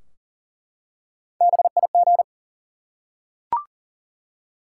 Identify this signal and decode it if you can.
Morse code, heard as a keyed tone, about 42 wpm, 700 Hz. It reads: BIG
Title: Three Letter Words - 40wpm